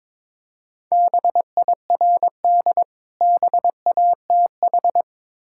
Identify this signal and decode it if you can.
Morse code, heard as a keyed tone, about 22 wpm, 690 Hz. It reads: BIRD BATH
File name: Difficult Words with Hints - 22wpm